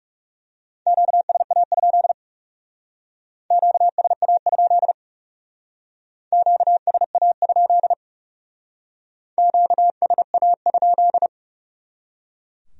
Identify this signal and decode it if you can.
Morse code, heard as a keyed tone, about 37 wpm, 700 Hz. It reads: QSA? QSA? QSA? QSA?
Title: Speed-Racing Q-Codes - 30wpm